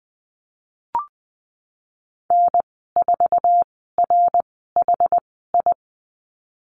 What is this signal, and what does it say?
Morse code, heard as a keyed tone, about 20 wpm, spaced as for 15 wpm, 700 Hz. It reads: N4RHI
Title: 069 Lesson 12g - Letter d - Call Signs 20wpm